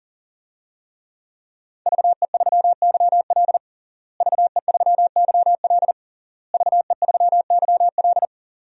Morse code, heard as a keyed tone, about 40 wpm, 700 Hz. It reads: VE3YL VE3YL VE3YL
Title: DXCC-100 Call Signs - 5 Characters - Repeated 3x - 40wpm